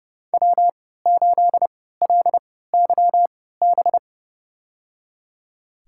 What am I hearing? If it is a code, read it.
Morse code, heard as a keyed tone, about 30 wpm, spaced as for 17 wpm, 700 Hz. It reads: W8LYB
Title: USA Call Signs - 5 Characters - 3x Character Spacing - 30wpm